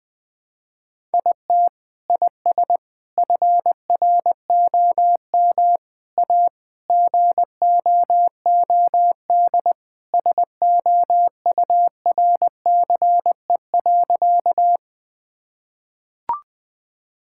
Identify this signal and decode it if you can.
Morse code, heard as a keyed tone, about 21 wpm, 700 Hz. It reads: IT IS FROM A GOOD SOURCE.
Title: Sentences from Top 1000 Words - Review Part 5 - 20wpm